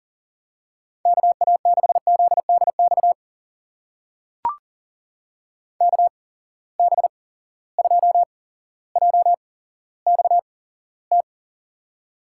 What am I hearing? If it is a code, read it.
Morse code, heard as a keyed tone, about 40 wpm, spaced as for 17 wpm, 700 Hz. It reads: KA6ZDX KB2JXT
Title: USA Call Signs - 8x Character Spacing - 40wpm